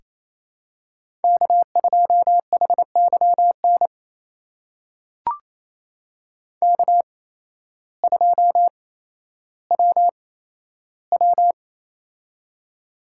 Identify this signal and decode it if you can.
Morse code, heard as a keyed tone, about 28 wpm, spaced as for 13 wpm, 700 Hz. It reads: K2HYN K2WW
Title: USA Call Signs - 8x Character Spacing - 28wpm